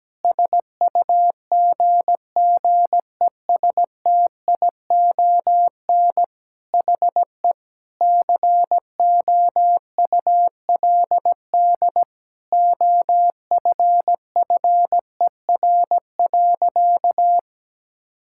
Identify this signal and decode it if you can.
Morse code, heard as a keyed tone, about 17 wpm, 700 Hz. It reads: SUGGESTION HE COULD OFFER.
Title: Sentences from Top 1600 Words 17wpm